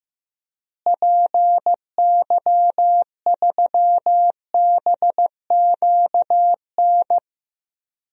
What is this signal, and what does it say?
Morse code, heard as a keyed tone, about 15 wpm, 700 Hz. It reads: PY3BQN
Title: DXCC-100 Call Signs - 6 Characters - Repeated 2x - 15wpm